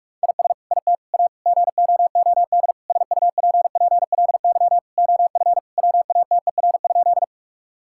Morse code, heard as a keyed tone, about 48 wpm, 700 Hz. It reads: IS IT A GOOD SUPPLY OF WATER?
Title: Sentences from Top 1000 Words - Review Part 5 - 45wpm